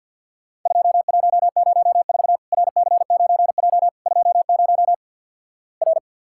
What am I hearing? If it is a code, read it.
Morse code, heard as a keyed tone, about 48 wpm, 700 Hz. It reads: 2104 RC9J 20 R
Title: Worked All Europe DX Contest - QTC Exchange with Repeats - 50wpm